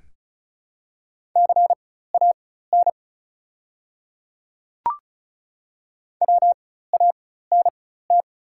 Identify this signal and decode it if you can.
Morse code, heard as a keyed tone, about 36 wpm, spaced as for 16 wpm, 700 Hz. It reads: CAN WANT
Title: 4x Character Spacing - Top 300 Words - 35wpm